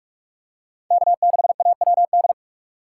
Morse code, heard as a keyed tone, about 41 wpm, 700 Hz. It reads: K6AWD